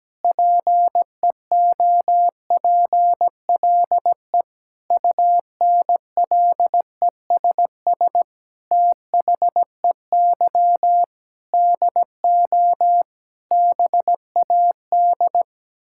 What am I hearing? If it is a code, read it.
Morse code, heard as a keyed tone, about 17 wpm, 700 Hz. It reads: PEOPLE UNLESS THEY DO BAD
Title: Sentences from Top 1000 Words - Review Part 6 - 17wpm